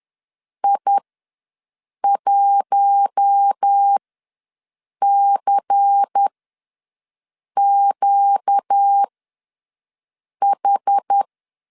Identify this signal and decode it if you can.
Morse code, heard as a keyed tone, about 10 wpm, spaced as for 5 wpm, 800 Hz. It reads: I1CQH
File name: LEZIONI / 6 PKX0 / 6 PK ws 10wpm 3745